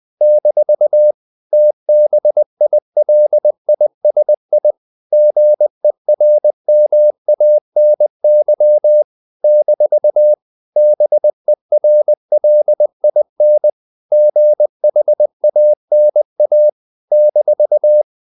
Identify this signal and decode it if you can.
Morse code, heard as a keyed tone, about 20 wpm, 600 Hz. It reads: - TBILISI GERMANY - BERLIN GHANA -
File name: CW / Practice / NationalCapitals20